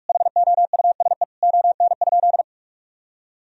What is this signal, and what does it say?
Morse code, heard as a keyed tone, about 46 wpm, 700 Hz. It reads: HOUSE ON?
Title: Sentences from Top 1000 Words - Review Part 4 - 45wpm